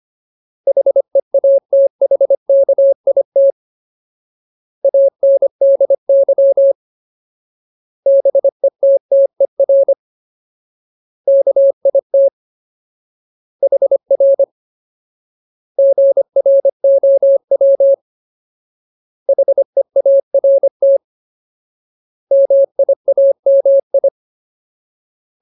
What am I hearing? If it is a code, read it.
Morse code, heard as a keyed tone, about 25 wpm, 550 Hz. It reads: HEATHKIT ANDY BETTER KIT HR GROW HEART MIAMI
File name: Ultimate Word List / 25WPM Ultimate